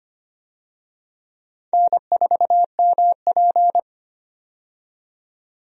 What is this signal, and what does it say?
Morse code, heard as a keyed tone, about 25 wpm, 700 Hz. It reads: N4MP
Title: USA Call Signs - 4 Characters - 25wpm